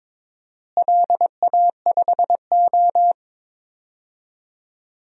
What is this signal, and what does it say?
Morse code, heard as a keyed tone, about 22 wpm, 700 Hz. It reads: LA5O
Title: DXCC-100 Call Signs - 4 Characters - Repeated 3x - 22wpm